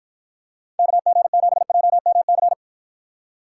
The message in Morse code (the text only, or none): KG7JMC